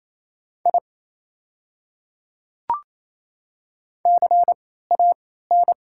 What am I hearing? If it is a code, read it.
Morse code, heard as a keyed tone, about 28 wpm, spaced as for 16 wpm, 700 Hz. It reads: I CAN
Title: 3x Character Spacing - Top 200 Words - 28wpm